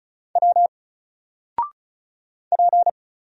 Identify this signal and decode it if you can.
Morse code, heard as a keyed tone, about 36 wpm, 700 Hz. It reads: W P
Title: Single Character (Letter:Number:Punctuation:Prosign) 35wpm